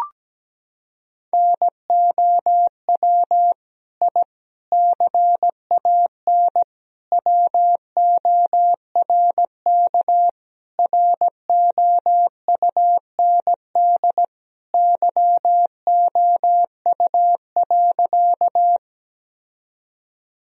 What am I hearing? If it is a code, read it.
Morse code, heard as a keyed tone, about 17 wpm, 700 Hz. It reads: NOW I CAN WORK ROUND YOU.
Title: Sentences from Top 1000 Words - Review Part 1 - 17wpm